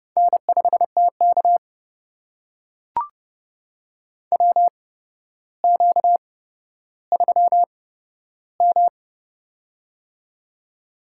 Morse code, heard as a keyed tone, about 32 wpm, spaced as for 13 wpm, 700 Hz. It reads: N5TK WQ3M
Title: USA Call Signs - 4 Characters - 8x Character Spacing - 30wpm